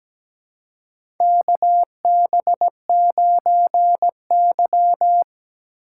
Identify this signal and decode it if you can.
Morse code, heard as a keyed tone, about 17 wpm, 700 Hz. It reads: KB9Y
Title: USA Call Signs - 4x Character Spacing - 17wpm